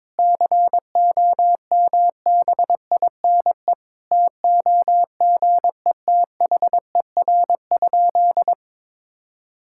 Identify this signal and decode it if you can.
Morse code, heard as a keyed tone, about 22 wpm, 700 Hz. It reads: COMBINE TOGETHER?
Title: Sentences from Top 1600 Words 22wpm